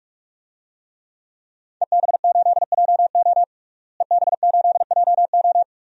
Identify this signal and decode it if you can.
Morse code, heard as a keyed tone, about 41 wpm, 700 Hz. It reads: EB8JO EB8JO
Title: DXCC-100 Call Signs - 5 Characters - Repeated 2x - 45wpm